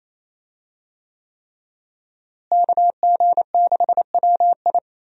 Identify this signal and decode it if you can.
Morse code, heard as a keyed tone, about 28 wpm, 700 Hz. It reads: KG6WI
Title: USA Call Signs - 5 Characters - Repeated 3x - 28wpm